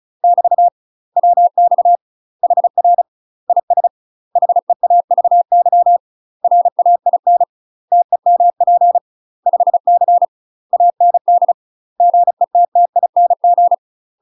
Morse code, heard as a keyed tone, about 33 wpm, 700 Hz. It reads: <BT> WX HR IS HEAVY RAIN TEMP 5C AND GETTING
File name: UK MASTER QSOs TRAINING / 35WPM / UK MASTER QSOs 0002